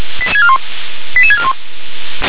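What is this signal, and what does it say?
Five-tone selective call (ZVEI), heard as quick runs of five tones, tones of about 70 ms. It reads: 88641
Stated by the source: ZVEI five-tone call received on SDR.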